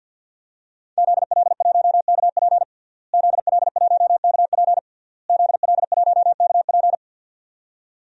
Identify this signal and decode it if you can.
Morse code, heard as a keyed tone, about 50 wpm, 700 Hz. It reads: ZL1KP ZL1KP ZL1KP
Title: DXCC-100 Call Signs - 5 Characters - Repeated 3x - 50wpm